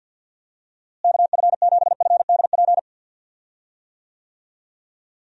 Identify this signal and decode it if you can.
Morse code, heard as a keyed tone, about 50 wpm, 700 Hz. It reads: KF7FDP
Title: USA Call Signs - 6 Characters - Repeated 3x - 50wpm